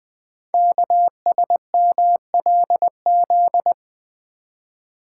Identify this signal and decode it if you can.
Morse code, heard as a keyed tone, about 20 wpm, 700 Hz. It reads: KSMLZ
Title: Code Groups - 5 Letters - 20wpm